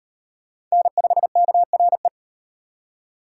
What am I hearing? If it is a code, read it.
Morse code, heard as a keyed tone, about 35 wpm, 700 Hz. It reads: N5KRE